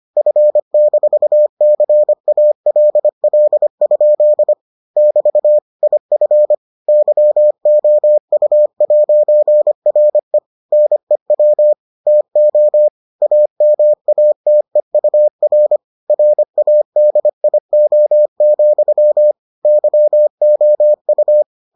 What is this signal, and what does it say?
Morse code, heard as a keyed tone, about 25 wpm, 600 Hz. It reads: F-CALL? <BT> IF YOU'RE NEW TO AMATEUR RADIO, YOU